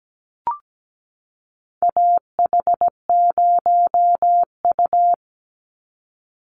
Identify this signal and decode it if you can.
Morse code, heard as a keyed tone, about 17 wpm, 700 Hz. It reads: AH0U
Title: Call Signs - 17wpm